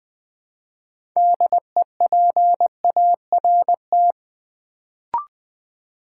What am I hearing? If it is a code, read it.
Morse code, heard as a keyed tone, about 20 wpm, 700 Hz. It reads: DEPART
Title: Build A Word - 3 Letters At A Time - 6-Letter Words - 20wpm